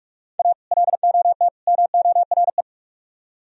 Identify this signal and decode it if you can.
Morse code, heard as a keyed tone, about 46 wpm, 700 Hz. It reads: A LOT MORE